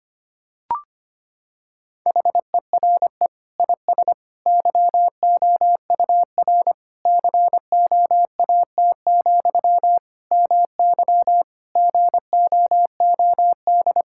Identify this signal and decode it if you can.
Morse code, heard as a keyed tone, about 25 wpm, 700 Hz. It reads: HERE IS YOUR COAT, MY GOOD
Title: Sentences from Top 1600 Words 25wpm